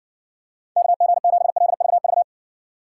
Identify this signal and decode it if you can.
Morse code, heard as a keyed tone, about 60 wpm, 700 Hz. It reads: KG7LVV